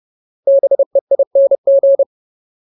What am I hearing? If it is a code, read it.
Morse code, heard as a keyed tone, about 30 wpm, 550 Hz. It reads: BEING
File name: Ultimate Word List / 30WPM Ultimate